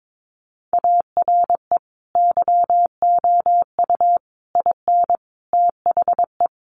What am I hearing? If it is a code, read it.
Morse code, heard as a keyed tone, about 22 wpm, 700 Hz. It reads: ARE YOU IN THE
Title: Sentences from Top 1200 Words 22wpm